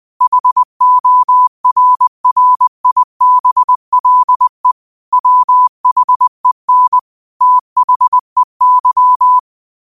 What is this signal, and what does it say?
Morse code, heard as a keyed tone, about 20 wpm, 1000 Hz. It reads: HORRIBLE WHEN THEY
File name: qotdfiles / 20wpm